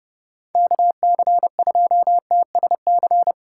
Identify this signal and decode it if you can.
Morse code, heard as a keyed tone, about 29 wpm, 700 Hz. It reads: KC2TSC